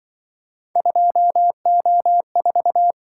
Morse code, heard as a keyed tone, about 25 wpm, 700 Hz. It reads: 2O4
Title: Speed-Racing - Three Letter-Number Combinations - 20wpm